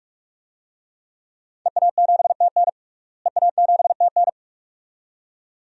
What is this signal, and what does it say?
Morse code, heard as a keyed tone, about 44 wpm, 700 Hz. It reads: EA7TN EA7TN
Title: DXCC-100 Call Signs - 5 Characters - Repeated 2x - 45wpm